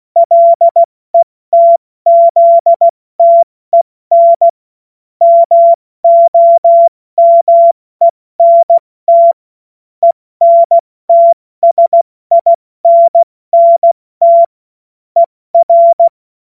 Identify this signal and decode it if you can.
Morse code, heard as a keyed tone, about 16 wpm, 680 Hz. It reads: LETZTEN MOMENT ENTSINNT ER
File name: Morsekurs / Witze / Witz 20. September 2022 70 F80